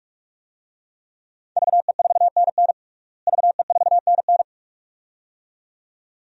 Morse code, heard as a keyed tone, about 45 wpm, 700 Hz. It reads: VE4NN VE4NN
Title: DXCC-100 Call Signs - 5 Characters - Repeated 2x - 45wpm